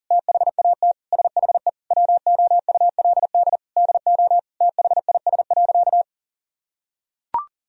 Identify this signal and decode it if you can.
Morse code, heard as a keyed tone, about 40 wpm, 700 Hz. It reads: THAT SHE WOULD DO THIS.
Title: Sentences from Top 1000 Words - Review Part 3 - 40wpm